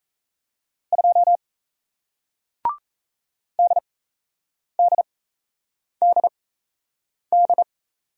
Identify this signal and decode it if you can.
Morse code, heard as a keyed tone, about 37 wpm, spaced as for 8 wpm, 700 Hz. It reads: 2 DDDD